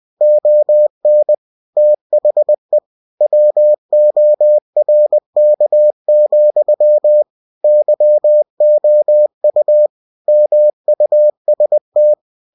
Morse code, heard as a keyed tone, about 20 wpm, 600 Hz. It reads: ON THE WORK, YOU MUST
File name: cw-books / winnie-the-pooh / winnie-20wpm-page-0115